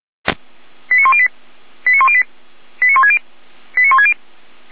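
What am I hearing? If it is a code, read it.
Five-tone selective call (ZVEI), heard as quick runs of five tones, tones of about 70 ms. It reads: 89108 89108 89160 89160
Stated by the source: ZVEI five-tone call received on SDR, from the feuerwehr-orlamuende.org server.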